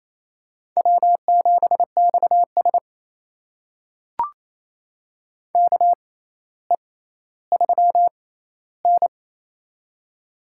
Morse code, heard as a keyed tone, about 28 wpm, spaced as for 14 wpm, 700 Hz. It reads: W7XS KE3N
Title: USA Call Signs - 6x Character Spacing - 28wpm